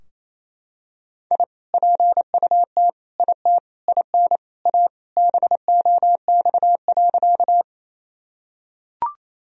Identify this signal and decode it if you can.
Morse code, heard as a keyed tone, about 28 wpm, 700 Hz. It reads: I PUT IT IN A BOX.
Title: Sentences from Top 1000 Words - Review Part 2 - 28wpm